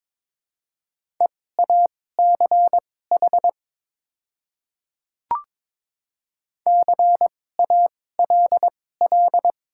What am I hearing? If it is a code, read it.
Morse code, heard as a keyed tone, about 22 wpm, spaced as for 16 wpm, 690 Hz. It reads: EACH CALL
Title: 2x Character Spacing - Top 100 Words - 22wpm